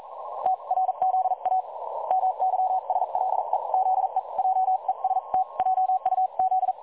Morse code, heard as a keyed tone, about 41 wpm, spaced as for 36 wpm, 750 Hz. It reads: TRZA MYSLE ZE JEST JUZ
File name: dlazaawansowanych / nagraniazpasma / websdr recording 3565.0kHz